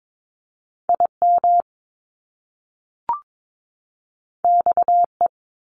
Morse code, heard as a keyed tone, about 22 wpm, 700 Hz. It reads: IM XE
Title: Top Two Letter Combinations in Words 22wpm